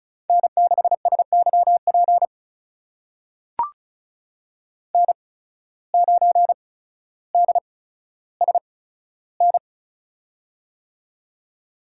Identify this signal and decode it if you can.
Morse code, heard as a keyed tone, about 36 wpm, spaced as for 15 wpm, 700 Hz. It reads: N6SYP N9DSN
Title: USA Call Signs - 8x Character Spacing - 35wpm